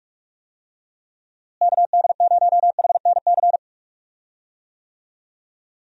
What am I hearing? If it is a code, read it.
Morse code, heard as a keyed tone, about 43 wpm, 700 Hz. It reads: KD0HNC
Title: USA Call Signs - 6 Characters - Repeated 2x - 45wpm